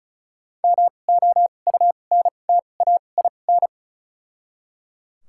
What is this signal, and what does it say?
Morse code, heard as a keyed tone, about 34 wpm, spaced as for 25 wpm, 700 Hz. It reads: MOUNTAIN
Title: 2x Character Spacing - Top 300 Words - 35wpm